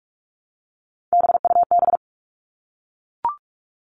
Morse code, heard as a keyed tone, about 44 wpm, 700 Hz. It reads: BUB
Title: All Cut Numbers - 3-Digit 45wpm